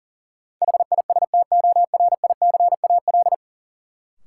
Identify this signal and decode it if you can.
Morse code, heard as a keyed tone, about 40 wpm, 700 Hz. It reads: HISTORICAL
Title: Sentences from Top 600 Words - New Word Drill - 40wpm